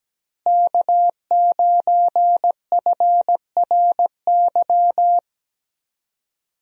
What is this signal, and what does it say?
Morse code, heard as a keyed tone, about 17 wpm, 700 Hz. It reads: K9FRY
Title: USA Call Signs - 5 Characters - Repeated 3x - 17wpm